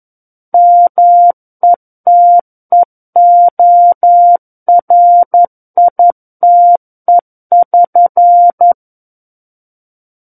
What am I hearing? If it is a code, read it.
Morse code, heard as a keyed tone, about 11 wpm, 700 Hz. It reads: METEORITE<SN>
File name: 11 WPM